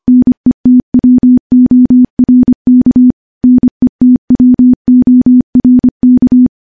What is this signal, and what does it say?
Morse code, heard as a keyed tone, about 25 wpm, 250 Hz. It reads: NETWORK NETWORK
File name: morse / morse-training-25WPM